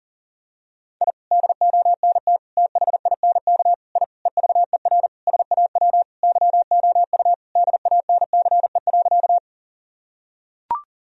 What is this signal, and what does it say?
Morse code, heard as a keyed tone, about 37 wpm, 700 Hz. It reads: I DONT THINK I EVER SAW YOU DANCE.
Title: Sentences from Top 1000 Words - Review Part 4 - 40wpm